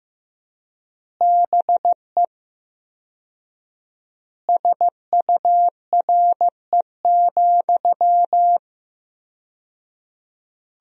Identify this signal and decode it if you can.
Morse code, heard as a keyed tone, about 16 wpm, 700 Hz. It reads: BE SURE,
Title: 4x Word-Spacing - Sentences from Top 500 Words - 15wpm